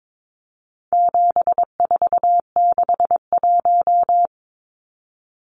Morse code, heard as a keyed tone, about 22 wpm, 700 Hz. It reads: 7461
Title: Numbers - 4-Digit 22wpm